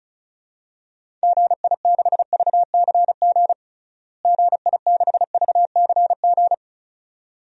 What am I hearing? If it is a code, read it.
Morse code, heard as a keyed tone, about 36 wpm, 700 Hz. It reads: GI6VCG GI6VCG
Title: DXCC-100 Call Signs - 6 Characters - Repeated 2x - 35wpm